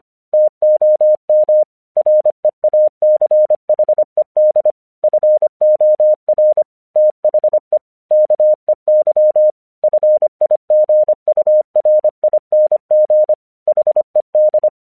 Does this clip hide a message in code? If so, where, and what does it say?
Yes: Morse code from 0.3 to 14.7 s, heard as a keyed tone, about 25 wpm, 610 Hz. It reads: TOM REACHED FOR THE KEY FIGURING HED